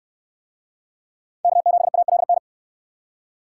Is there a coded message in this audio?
Yes: Morse code from 1.4 to 2.4 s, heard as a keyed tone, about 68 wpm, 700 Hz. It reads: K7ALN